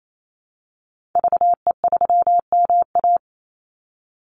Morse code, heard as a keyed tone, about 28 wpm, 700 Hz. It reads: VE3MA
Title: Call Signs Repeated 3x - 28wpm